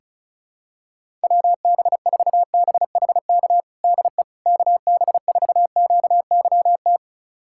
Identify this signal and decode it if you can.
Morse code, heard as a keyed tone, about 34 wpm, 700 Hz. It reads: WB4BHK DE KB4QYT
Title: USA Callsign DE Callsign - 6 Characters - 35wpm